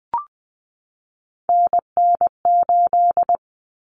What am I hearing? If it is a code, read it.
Morse code, heard as a keyed tone, about 20 wpm, 700 Hz. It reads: NN8
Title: Common Cut Numbers - 3-Digit 20wpm